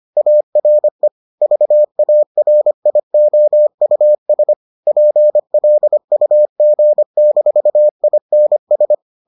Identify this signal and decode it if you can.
Morse code, heard as a keyed tone, about 25 wpm, 600 Hz. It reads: ARE VARIOUS PLUG-INS